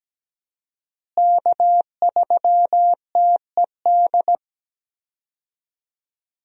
Morse code, heard as a keyed tone, about 17 wpm, 700 Hz. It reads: K3TED
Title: USA Call Signs - 5 Characters - Repeated 2x - 17wpm